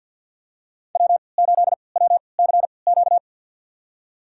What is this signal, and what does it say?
Morse code, heard as a keyed tone, about 46 wpm, spaced as for 27 wpm, 690 Hz. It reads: W8WCQ
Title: USA Call Signs - 5 Characters - 3x Character Spacing - 50wpm